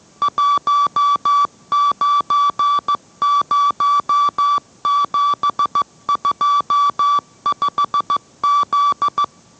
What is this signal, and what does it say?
Morse code, heard as a keyed tone, about 15 wpm, 1150 Hz. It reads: 190725Z